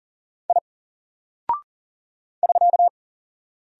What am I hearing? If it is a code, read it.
Morse code, heard as a keyed tone, about 38 wpm, 700 Hz. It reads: I <SK>
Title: Single Character (Letter:Number:Punctuation:Prosign) 40wpm